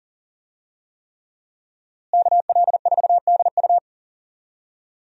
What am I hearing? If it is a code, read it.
Morse code, heard as a keyed tone, about 37 wpm, 700 Hz. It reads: KL4DU
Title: USA Call Signs - 5 Characters - Repeated 3x - 40wpm